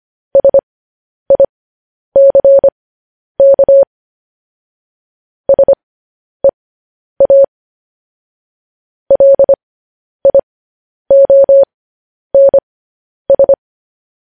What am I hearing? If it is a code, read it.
Morse code, heard as a keyed tone, about 25 wpm, spaced as for 10 wpm, 550 Hz. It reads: SICK SEA LIONS
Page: news headlines in morse code at 10 wpm